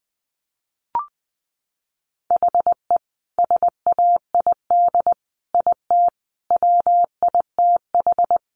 Morse code, heard as a keyed tone, about 19 wpm, 700 Hz. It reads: HE SAID IT WITH
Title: Sentences from Top 1200 Words 20wpm